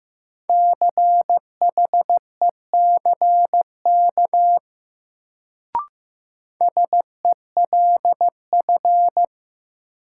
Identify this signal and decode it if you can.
Morse code, heard as a keyed tone, about 15 wpm, 700 Hz. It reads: CHECK SELF
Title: Sentences from Top 1400 Words - New Word Drill Alt - 15wpm